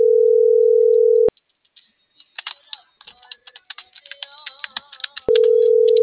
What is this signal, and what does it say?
Ringback tone (North America), 440 + 480 Hz.